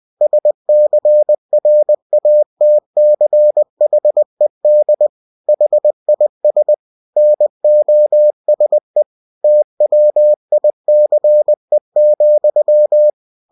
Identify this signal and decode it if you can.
Morse code, heard as a keyed tone, about 20 wpm, 600 Hz. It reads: SCRATCHED HIS NOSE TWICE,
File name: cw-books / winnie-the-pooh / winnie-20wpm-page-0025